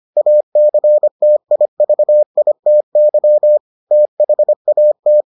Morse code, heard as a keyed tone, about 25 wpm, 600 Hz. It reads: ACTIVITY THAT